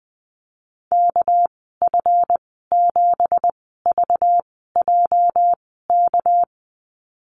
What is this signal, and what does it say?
Morse code, heard as a keyed tone, about 20 wpm, spaced as for 15 wpm, 700 Hz. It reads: KF7VJK
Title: 231 Lesson 37g - Letter x - Call Signs 20wpm